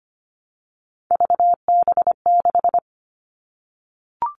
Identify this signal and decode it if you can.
Morse code, heard as a keyed tone, about 25 wpm, 700 Hz. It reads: VB6
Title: Three Challenging Letter-Number Combinations 25wpm